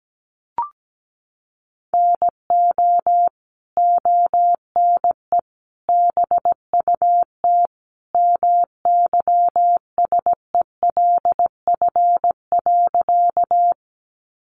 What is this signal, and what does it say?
Morse code, heard as a keyed tone, about 17 wpm, 700 Hz. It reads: NO ONE BUT MYSELF.
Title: Sentences from Top 1300 Words 17wpm